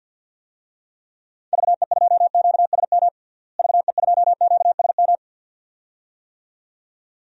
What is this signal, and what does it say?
Morse code, heard as a keyed tone, about 52 wpm, 700 Hz. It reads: VE2QSM VE2QSM
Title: DXCC-100 Call Signs - 6 Characters - Repeated 2x - 50wpm